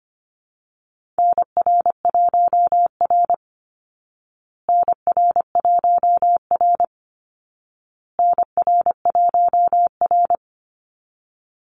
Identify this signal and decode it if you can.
Morse code, heard as a keyed tone, about 25 wpm, 700 Hz. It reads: NR1R NR1R NR1R
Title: Call Signs Repeated 2x - 25wpm